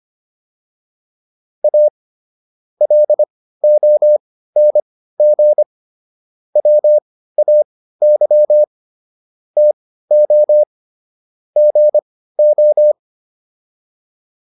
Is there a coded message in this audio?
Yes: Morse code from 1.6 to 12.9 s, heard as a keyed tone, about 25 wpm, spaced as for 15 wpm, 600 Hz. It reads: A LONG WAY TO GO